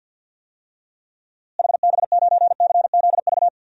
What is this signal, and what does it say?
Morse code, heard as a keyed tone, about 52 wpm, 700 Hz. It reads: HB9CZV
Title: DXCC-100 Call Signs - 6 Characters - Repeated 3x - 50wpm